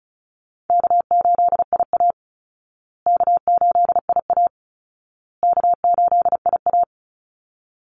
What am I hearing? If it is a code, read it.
Morse code, heard as a keyed tone, about 34 wpm, 700 Hz. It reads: K8IA K8IA K8IA
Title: Easy Call Signs Repeated 3x - 35wpm